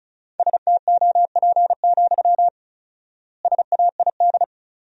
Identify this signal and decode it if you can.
Morse code, heard as a keyed tone, about 36 wpm, 700 Hz. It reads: STOP, SAID
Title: 4x Word-Spacing - Sentences from Top 300 Words - 35wpm